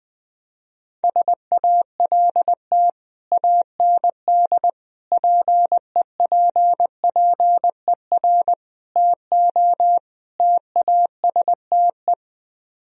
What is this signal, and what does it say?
Morse code, heard as a keyed tone, about 20 wpm, 700 Hz. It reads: SALT AND PEPPER TO TASTE
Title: Speed-Racing - Sets of 5 Words 20wpm